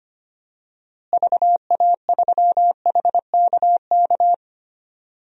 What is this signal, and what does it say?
Morse code, heard as a keyed tone, about 25 wpm, 700 Hz. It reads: VA3HKK
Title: DXCC-100 Call Signs - 6 Characters - Repeated 2x - 25wpm